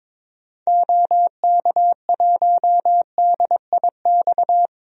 Morse code, heard as a keyed tone, about 22 wpm, 700 Hz. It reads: OK1DIX
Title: DXCC-100 Call Signs - 6 Characters - 22wpm